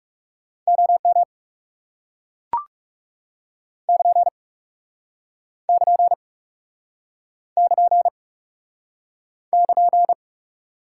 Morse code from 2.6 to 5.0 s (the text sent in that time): <KN>